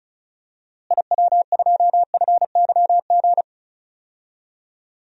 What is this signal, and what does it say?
Morse code, heard as a keyed tone, about 36 wpm, 700 Hz. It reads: IW2FYG
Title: DXCC-100 Call Signs - 6 Characters - Repeated 3x - 35wpm